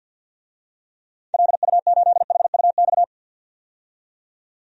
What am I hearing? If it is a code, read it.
Morse code, heard as a keyed tone, about 48 wpm, 700 Hz. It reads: LU8HUX
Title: DXCC-100 Call Signs - 6 Characters - 50wpm